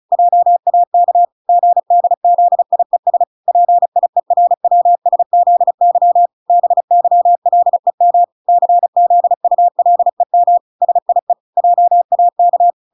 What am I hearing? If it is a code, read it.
Morse code, heard as a keyed tone, about 34 wpm, 700 Hz. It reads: JAK GDZIES PIERWSZY BYLEM CZULEM SIE JAK